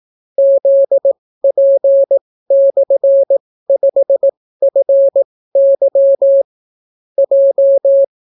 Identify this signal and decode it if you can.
Morse code, heard as a keyed tone, about 18 wpm, spaced as for 15 wpm, 550 Hz. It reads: ZP/5FY J